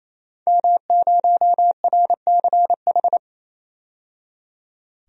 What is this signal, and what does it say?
Morse code, heard as a keyed tone, about 28 wpm, 700 Hz. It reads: M0RCH
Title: DXCC-100 Call Signs - 5 Characters - 28wpm